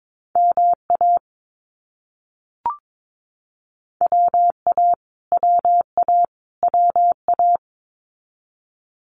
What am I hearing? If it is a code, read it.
Morse code, heard as a keyed tone, about 22 wpm, 700 Hz. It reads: MA WA WA WA
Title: US States Repeated 3x 22wpm